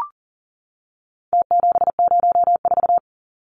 Morse code, heard as a keyed tone, about 40 wpm, 700 Hz. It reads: T704
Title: Numbers - Contest Style - 4-Digit 40wpm